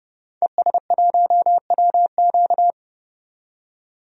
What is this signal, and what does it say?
Morse code, heard as a keyed tone, about 30 wpm, 700 Hz. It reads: ES1WQ
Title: DXCC-100 Call Signs - 5 Characters - 30wpm